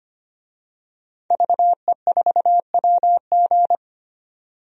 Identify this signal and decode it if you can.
Morse code, heard as a keyed tone, about 25 wpm, 700 Hz. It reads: VE4WG
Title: DXCC-100 Call Signs - 5 Characters - 25wpm